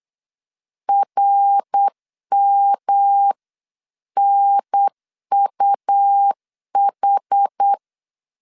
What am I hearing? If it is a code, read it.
Morse code, heard as a keyed tone, about 8 wpm, 800 Hz. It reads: RMNUH